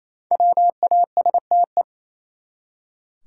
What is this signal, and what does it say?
Morse code, heard as a keyed tone, about 28 wpm, 700 Hz. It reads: WASTE
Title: Five Letter Words - 28wpm